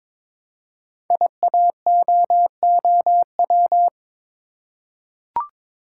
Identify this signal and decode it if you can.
Morse code, heard as a keyed tone, about 23 wpm, 700 Hz. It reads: IAOOW
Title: Code Groups - 5 Letters - 22wpm